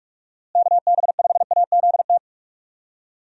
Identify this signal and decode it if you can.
Morse code, heard as a keyed tone, about 42 wpm, 700 Hz. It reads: KB5AZT
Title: USA Call Signs - 6 Characters - Repeated 3x - 45wpm